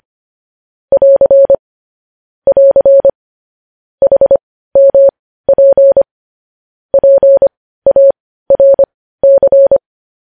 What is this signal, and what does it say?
Morse code, heard as a keyed tone, about 25 wpm, spaced as for 15 wpm, 550 Hz. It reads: <AR> <AR> HMP PARC
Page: news headlines in morse code at 15 wpm